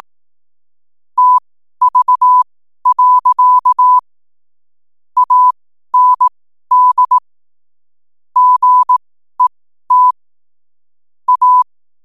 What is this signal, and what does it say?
Morse code, heard as a keyed tone, about 18 wpm, spaced as for 12 wpm, 1000 Hz. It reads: TV. AND GET A
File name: n7net / 12wpm